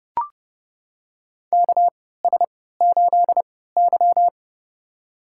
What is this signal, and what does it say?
Morse code, heard as a keyed tone, about 30 wpm, spaced as for 17 wpm, 700 Hz. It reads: KS8Y
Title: USA Call Signs - 4 Characters - 3x Character Spacing - 30wpm